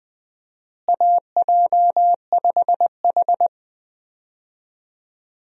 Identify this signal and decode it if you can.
Morse code, heard as a keyed tone, about 21 wpm, 700 Hz. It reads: AJ5H
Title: USA Call Signs - 4 Characters - Repeated 2x - 20wpm